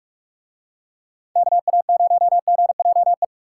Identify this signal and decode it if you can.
Morse code, heard as a keyed tone, about 44 wpm, 700 Hz. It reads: KA0GJE